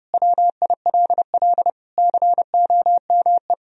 Morse code, heard as a keyed tone, about 30 wpm, 700 Hz. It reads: WILL COME
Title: Sentences from Top 1000 Words - Review Part 1 - 30wpm